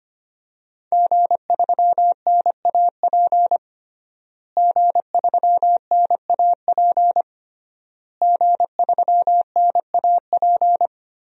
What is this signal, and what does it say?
Morse code, heard as a keyed tone, about 25 wpm, 700 Hz. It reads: G3NAP G3NAP G3NAP
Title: DXCC-100 Call Signs - 5 Characters - Repeated 3x - 25wpm